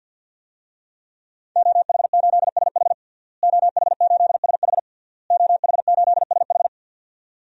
Morse code, heard as a keyed tone, about 51 wpm, 700 Hz. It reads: OH8SH OH8SH OH8SH
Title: DXCC-100 Call Signs - 5 Characters - Repeated 3x - 50wpm